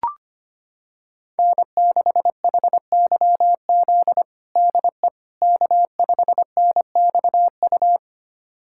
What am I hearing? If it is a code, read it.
Morse code, heard as a keyed tone, about 25 wpm, 700 Hz. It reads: N6HYZ DE K5NXU
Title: USA Callsign DE Callsign - 5 Characters - 25wpm